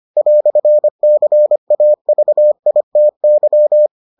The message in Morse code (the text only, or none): "CAVITY